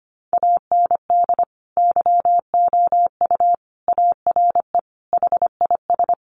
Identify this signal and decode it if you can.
Morse code, heard as a keyed tone, about 25 wpm, 700 Hz. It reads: AND YOU ARE HIS
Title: Sentences from Top 1300 Words 25wpm